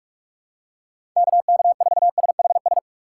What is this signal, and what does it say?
Morse code, heard as a keyed tone, about 47 wpm, 700 Hz. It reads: KK4SHS
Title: USA Call Signs - 6 Characters - 8x Character Spacing - 45wpm